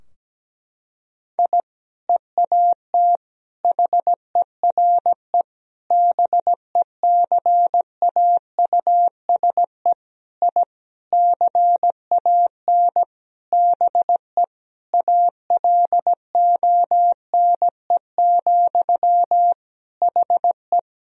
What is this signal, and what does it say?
Morse code, heard as a keyed tone, about 17 wpm, 700 Hz. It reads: I EAT HERE BECAUSE I CAN BE ALONE, HE
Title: Sentences from Top 1000 Words - Review Part 3 - 17wpm